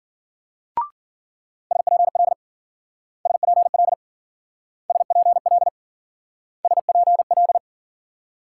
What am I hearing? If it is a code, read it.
Morse code, heard as a keyed tone, about 48 wpm, 700 Hz. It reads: SPL SPL SPL SPL
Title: Speed-Racing Multi-letter Phonemes - 40wpm